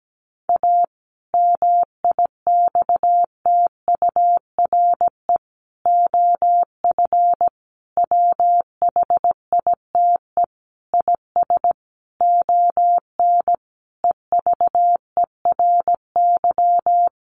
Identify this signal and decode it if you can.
Morse code, heard as a keyed tone, about 17 wpm, 700 Hz. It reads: A MIXTURE OF WHITE IS ON EVERY
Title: Sentences from Top 1500 Words 17wpm